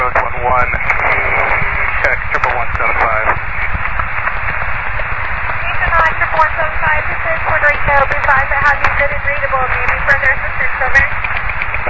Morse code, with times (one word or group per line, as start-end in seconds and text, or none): none